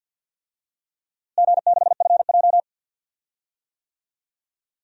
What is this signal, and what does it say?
Morse code, heard as a keyed tone, about 47 wpm, 700 Hz. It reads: G6FJ
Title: DXCC-100 Call Signs - 4 Characters - Repeated 3x - 50wpm